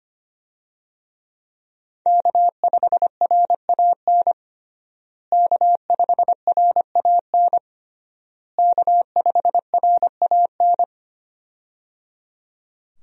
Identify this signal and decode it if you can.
Morse code, heard as a keyed tone, about 25 wpm, 700 Hz. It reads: K5RAN K5RAN K5RAN
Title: USA Call Signs - 5 Characters - Repeated 3x - 25wpm